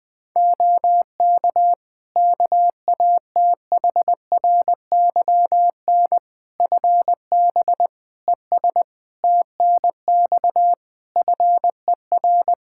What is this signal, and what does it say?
Morse code, heard as a keyed tone, about 20 wpm, 700 Hz. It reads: OK KATHRYN FB ES TNX FER
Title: K1USN SST QSO Practice 20wpm